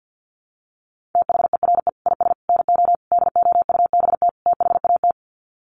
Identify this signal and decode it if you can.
Morse code, heard as a keyed tone, about 49 wpm, 700 Hz. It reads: THERE IS NO DOUBT THAT
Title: Sets of 5 Words 50wpm